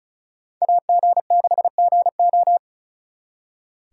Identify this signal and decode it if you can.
Morse code, heard as a keyed tone, about 45 wpm, 700 Hz. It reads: AG6GO